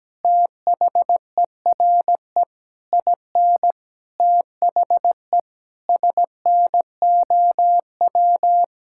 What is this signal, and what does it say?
Morse code, heard as a keyed tone, about 17 wpm, 700 Hz. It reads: THERE IN THE SNOW
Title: Sentences from Top 1600 Words 17wpm